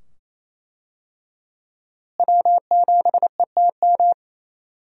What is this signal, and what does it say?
Morse code, heard as a keyed tone, about 27 wpm, 700 Hz. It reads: W7ETM
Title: USA Call Signs - 5 Characters - Repeated 2x - 28wpm